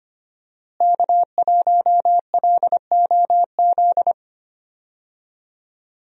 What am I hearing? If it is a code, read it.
Morse code, heard as a keyed tone, about 25 wpm, 700 Hz. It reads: K1LOZ
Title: USA Call Signs - 5 Characters - Repeated 3x - 25wpm